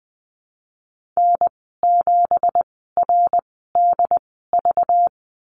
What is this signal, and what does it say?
Morse code, heard as a keyed tone, about 20 wpm, spaced as for 15 wpm, 700 Hz. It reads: N7RDV
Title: 193 Lesson 31g - Letter v - Call Signs 20wpm